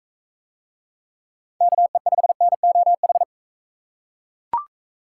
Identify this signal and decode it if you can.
Morse code, heard as a keyed tone, about 40 wpm, 700 Hz. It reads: KE5NOH